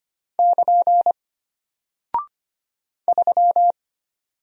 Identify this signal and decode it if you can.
Morse code, heard as a keyed tone, about 25 wpm, 700 Hz. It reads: <KN> 3
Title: Single Character (Letter:Number:Punctuation:Prosign) Alt 25wpm